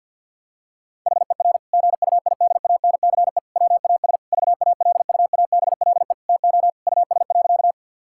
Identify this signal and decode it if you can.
Morse code, heard as a keyed tone, about 52 wpm, 700 Hz. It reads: HER GUIDANCE WAS VALUABLE TO US.
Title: Sentences from Top 1600 Words 50wpm